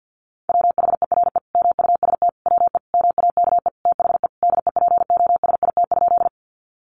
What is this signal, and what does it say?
Morse code, heard as a keyed tone, about 50 wpm, 700 Hz. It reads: WHERE MUST WE MAKE THE DEPOSIT?
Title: Sentences from Top 1300 Words 50wpm